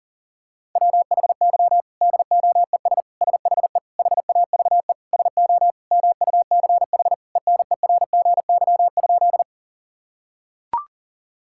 Morse code, heard as a keyed tone, about 40 wpm, 700 Hz. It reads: WHY DOES SHE HAVE SO MUCH ENERGY?
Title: Sentences from Top 1000 Words - Review Part 6 - 40wpm